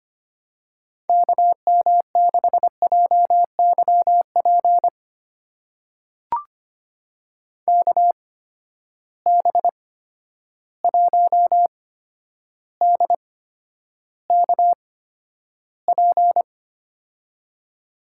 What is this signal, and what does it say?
Morse code, heard as a keyed tone, about 25 wpm, spaced as for 11 wpm, 700 Hz. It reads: KM6JYP KB1DKP
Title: USA Call Signs - 6 Characters - 8x Character Spacing - 25wpm